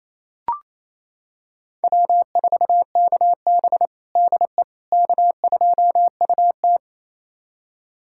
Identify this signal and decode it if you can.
Morse code, heard as a keyed tone, about 28 wpm, 700 Hz. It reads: W4KB DE K2UT
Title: USA Callsign DE Callsign - 4 Characters - 28wpm